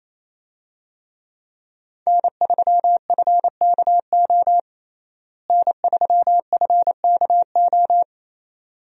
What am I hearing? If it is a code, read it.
Morse code, heard as a keyed tone, about 28 wpm, 700 Hz. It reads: N3FKO N3FKO
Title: USA Call Signs - 5 Characters - Repeated 2x - 28wpm